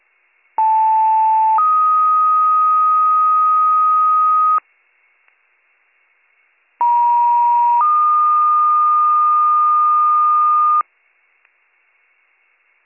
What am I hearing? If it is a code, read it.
Two-tone page, heard as a one-second tone, then a three-second tone, about 880 then 1250 Hz, about 950 then 1220 Hz. It is heard from 0.6 to 10.8 s.